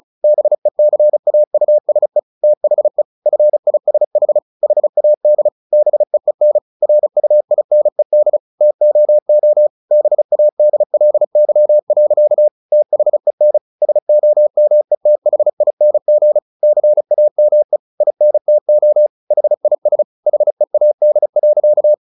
Morse code, heard as a keyed tone, about 34 wpm, 600 Hz. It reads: BECAUSE THE FISH HAD BEEN RUINED TOO BADLY. THEN SOMETHING CAME INTO HIS HEAD.
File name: the old man and the sea 2 35wpm